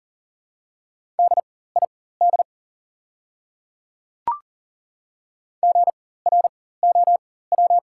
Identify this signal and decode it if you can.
Morse code, heard as a keyed tone, about 37 wpm, spaced as for 18 wpm, 700 Hz. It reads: DID GROW
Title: 4x Character Spacing - Top 300 Words - 40wpm